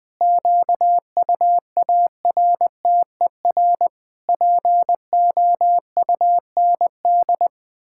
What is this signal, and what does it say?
Morse code, heard as a keyed tone, about 20 wpm, 700 Hz. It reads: QUARTER POUND
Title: Pangram Challenge v2 - 20wpm